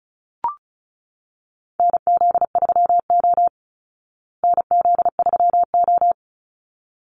Morse code, heard as a keyed tone, about 34 wpm, 700 Hz. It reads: NZ3O NZ3O
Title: Call Signs Repeated 3x - 35wpm